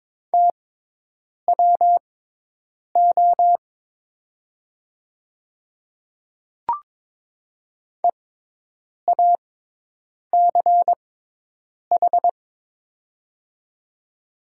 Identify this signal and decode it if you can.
Morse code, heard as a keyed tone, about 22 wpm, spaced as for 8 wpm, 700 Hz. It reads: TWO EACH
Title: 6x Character Spacing - Top 200 Words - 22wpm